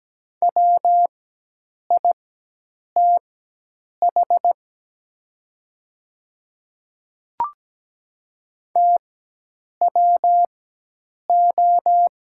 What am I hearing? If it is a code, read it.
Morse code, heard as a keyed tone, about 17 wpm, spaced as for 8 wpm, 700 Hz. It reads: WITH TWO
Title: 4x Character Spacing - Top 100 Words - 17wpm